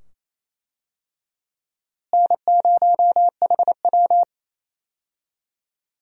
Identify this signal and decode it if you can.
Morse code, heard as a keyed tone, about 28 wpm, 700 Hz. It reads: N0HW
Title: USA Call Signs - 4 Characters - Repeated 2x - 28wpm